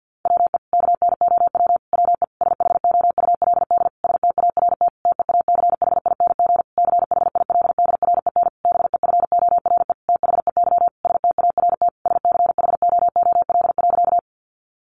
Morse code, heard as a keyed tone, about 48 wpm, 700 Hz. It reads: WE KNOW WE SHOULD START TEACHING CHILDREN BEFORE THEY START SCHOOL.
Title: Sentences from Top 1100 Words 50wpm